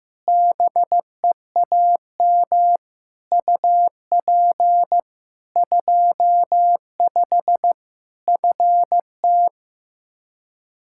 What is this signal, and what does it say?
Morse code, heard as a keyed tone, about 15 wpm, 700 Hz. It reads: BEAM UP 25 FT
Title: K1USN SST QSO Practice 15wpm